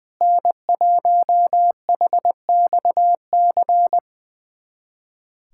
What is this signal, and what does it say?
Morse code, heard as a keyed tone, about 20 wpm, 700 Hz. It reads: N1HXC
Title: USA Callsign DE Callsign - 5 Characters - 20wpm